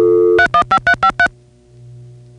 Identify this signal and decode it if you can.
DTMF, heard as telephone touch tones, tones of about 80 ms. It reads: 315353